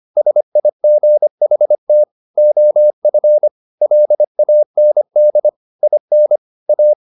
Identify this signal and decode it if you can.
Morse code, heard as a keyed tone, about 25 wpm, 600 Hz. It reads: SIGHT OF LAND IN A
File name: the old man and the sea part 1 25wpm